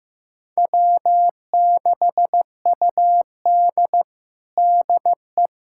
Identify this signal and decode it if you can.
Morse code, heard as a keyed tone, about 15 wpm, 700 Hz. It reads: W6UD DE
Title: USA Callsign DE Callsign - 4 Characters - 15wpm